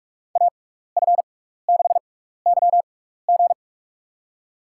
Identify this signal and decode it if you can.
Morse code, heard as a keyed tone, about 44 wpm, spaced as for 15 wpm, 690 Hz. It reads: AF6YG
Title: USA Call Signs - 5 Characters - 6x Character Spacing - 45wpm